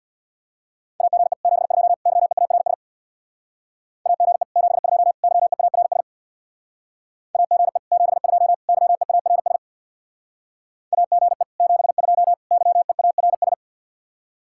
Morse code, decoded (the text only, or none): AGE 72 YEARS AGE 72 YEARS AGE 72 YEARS AGE 72 YEARS